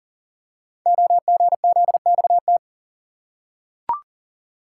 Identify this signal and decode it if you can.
Morse code, heard as a keyed tone, about 42 wpm, 690 Hz. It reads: OGZXT
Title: Code Groups - 5 Letters - 40wpm